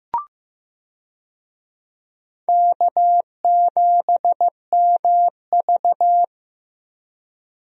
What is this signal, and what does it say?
Morse code, heard as a keyed tone, about 15 wpm, 700 Hz. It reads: K7MV
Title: USA Call Signs - 4 Characters - Repeated 2x - 15wpm